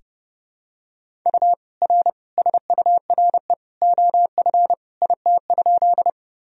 Morse code, heard as a keyed tone, about 30 wpm, 700 Hz. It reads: U R SURE OF IT?
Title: Sentences from Top 300 Words and Abbreviations 30wpm